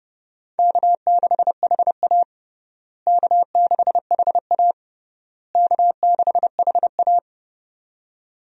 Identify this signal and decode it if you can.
Morse code, heard as a keyed tone, about 30 wpm, 700 Hz. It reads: K6HA K6HA K6HA
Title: USA Call Signs - 4 Characters - Repeated 3x - 30wpm